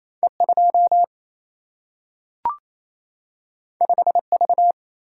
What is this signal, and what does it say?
Morse code, heard as a keyed tone, about 28 wpm, 700 Hz. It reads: E 2 5 V
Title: Single-Letter Single-Number 28wpm